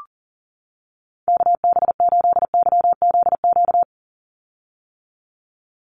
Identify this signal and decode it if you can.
Morse code, heard as a keyed tone, about 40 wpm, 700 Hz. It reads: KB8YZQ
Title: USA Call Signs 40wpm